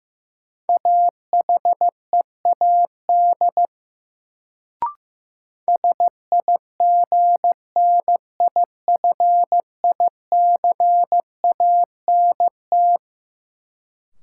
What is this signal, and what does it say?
Morse code, heard as a keyed tone, about 15 wpm, 700 Hz. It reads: AHEAD SIGNIFICANT
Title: Sentences from Top 1000 Words - Review Part 6 - New Word Drill - 15wpm